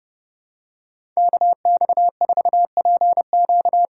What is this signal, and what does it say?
Morse code, heard as a keyed tone, about 30 wpm, 700 Hz. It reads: KX4PQ
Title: K1USN SST QSO Practice - 30wpm